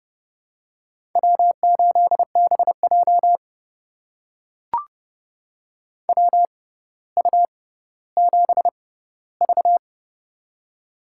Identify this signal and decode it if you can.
Morse code, heard as a keyed tone, about 30 wpm, spaced as for 15 wpm, 700 Hz. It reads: W8BJ WU7V
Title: USA Call Signs - 6x Character Spacing - 30wpm